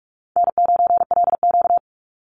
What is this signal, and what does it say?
Morse code, heard as a keyed tone, about 44 wpm, 700 Hz. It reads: N9LQ
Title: Easy Call Signs - 45wpm